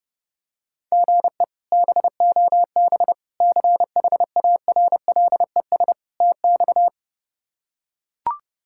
Morse code, heard as a keyed tone, about 31 wpm, 700 Hz. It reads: GE BOB CHARLES TX
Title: K1USN SST QSO Practice - 30wpm